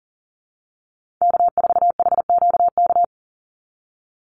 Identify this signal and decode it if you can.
Morse code, heard as a keyed tone, about 38 wpm, 700 Hz. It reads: K4HQK
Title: Call Signs Repeated 2x - 40wpm